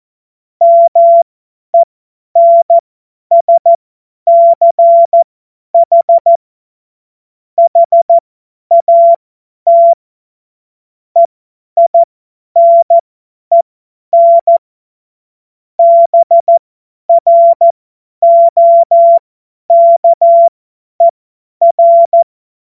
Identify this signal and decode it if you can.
Morse code, heard as a keyed tone, about 14 wpm, spaced as for 10 wpm, 680 Hz. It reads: MENSCH HAT EINEN BROKER
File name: Witz-01-Juli-2025-50-F70